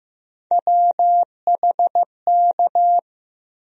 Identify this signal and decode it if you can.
Morse code, heard as a keyed tone, about 15 wpm, 700 Hz. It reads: WHK
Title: USA Call Sign Postfixes - 3 Letters 15wpm